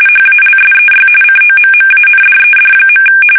RTTY, about 45.45 baud. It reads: HALSEY COMMANDED.
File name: fsk850 / battleshipnj